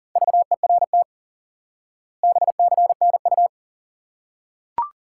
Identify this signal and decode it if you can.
Morse code, heard as a keyed tone, about 40 wpm, 700 Hz. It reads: VERT BCNU
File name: IQE Course 3 - Lesson 89 - YAGI - Two Elements - 40wpm